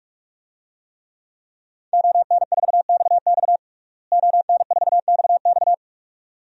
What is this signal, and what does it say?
Morse code, heard as a keyed tone, about 42 wpm, 700 Hz. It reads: ON4XX ON4XX
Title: DXCC-100 Call Signs - 5 Characters - Repeated 2x - 45wpm